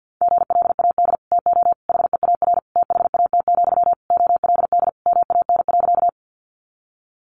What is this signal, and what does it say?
Morse code, heard as a keyed tone, about 50 wpm, 700 Hz. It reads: GLAD TO HEAR THAT, OLD MAN.
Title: Sentences from Top 800 Words 50wpm